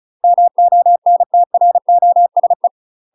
Morse code, heard as a keyed tone, about 34 wpm, 700 Hz. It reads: MONTROSE
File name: UK MASTER QSOs TRAINING/35WPM